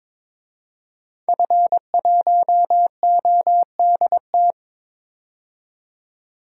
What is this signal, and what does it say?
Morse code, heard as a keyed tone, about 21 wpm, 700 Hz. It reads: F1ODT
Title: DXCC-100 Call Signs - 5 Characters - 22wpm